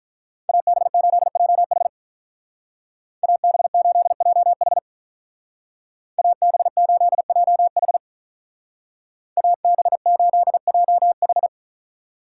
Morse code, heard as a keyed tone, about 46 wpm, 700 Hz. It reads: AB8JH AB8JH AB8JH AB8JH